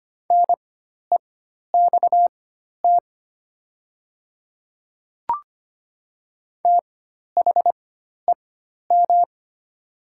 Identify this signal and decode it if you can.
Morse code, heard as a keyed tone, about 25 wpm, spaced as for 12 wpm, 690 Hz. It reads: NEXT THEM
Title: 4x Character Spacing - Top 300 Words - 25wpm